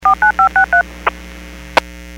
DTMF, heard as telephone touch tones, tones of about 110 ms. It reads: 49563
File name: WPS 451.25 WPS